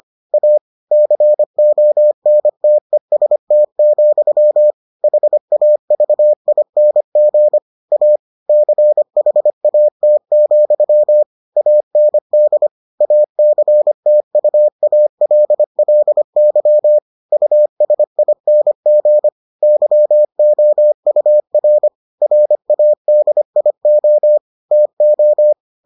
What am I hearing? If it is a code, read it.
Morse code, heard as a keyed tone, about 25 wpm, 600 Hz. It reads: A CONTEST, HAVING A CHAT, AND ACTUALLY USING YOUR RADIO TO